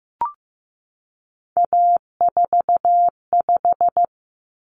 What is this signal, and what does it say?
Morse code, heard as a keyed tone, about 15 wpm, 700 Hz. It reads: A45
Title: Common Cut Numbers - 3-Digit 15wpm